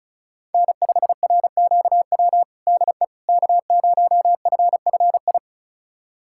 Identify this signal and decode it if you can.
Morse code, heard as a keyed tone, about 36 wpm, 700 Hz. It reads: N5RQW DE K0FFI
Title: USA Callsign DE Callsign 35wpm